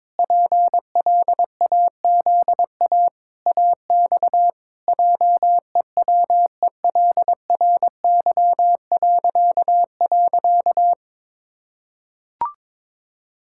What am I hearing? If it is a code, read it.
Morse code, heard as a keyed tone, about 21 wpm, 700 Hz. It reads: PLAZA AX JEWELRY..
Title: Pangram Challenge v2 - 22wpm